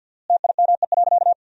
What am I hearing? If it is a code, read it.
Morse code, heard as a keyed tone, about 50 wpm, 700 Hz. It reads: TIME.
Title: 4x Word-Spacing - Sentences from Top 400 Words - 50wpm